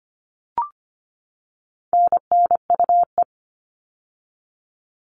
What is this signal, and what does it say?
Morse code, heard as a keyed tone, about 25 wpm, 700 Hz. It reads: NNUE
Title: All Cut Numbers - 4-Digit 25wpm